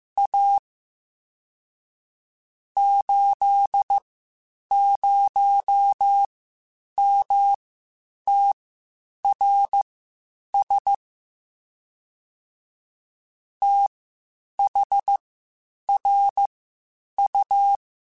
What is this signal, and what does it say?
Morse code, heard as a keyed tone, about 15 wpm, spaced as for 8 wpm, 800 Hz. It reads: A 80MTRS THRU
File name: (spaced) / QSOs Spaced-mp3s / WD0FBT 1 @13 sp